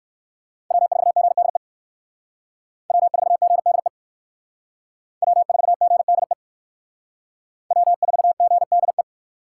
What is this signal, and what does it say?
Morse code, heard as a keyed tone, about 56 wpm, 700 Hz. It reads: W4GDE W4GDE W4GDE W4GDE